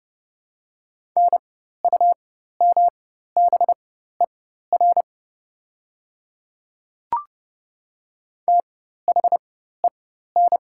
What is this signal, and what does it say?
Morse code, heard as a keyed tone, about 30 wpm, spaced as for 14 wpm, 700 Hz. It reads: NUMBER THEN
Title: 4x Character Spacing - Top 200 Words - 30wpm